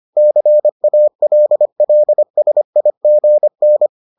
Morse code, heard as a keyed tone, about 25 wpm, 600 Hz. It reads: CALLSIGN